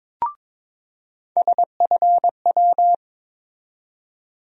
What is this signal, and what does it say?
Morse code, heard as a keyed tone, about 22 wpm, 700 Hz. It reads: SFW
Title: USA Call Sign Postfixes - 3 Letters 22wpm